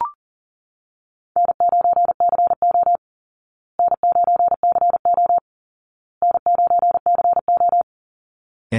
Morse code, heard as a keyed tone, about 40 wpm, 700 Hz. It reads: N9CO N9CO N9CO
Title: Call Signs Repeated 2x - 40wpm